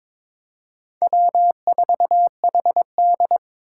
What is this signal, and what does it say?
Morse code, heard as a keyed tone, about 22 wpm, 700 Hz. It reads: W4HD
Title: Four Challenging Letter-Number Combinations 22wpm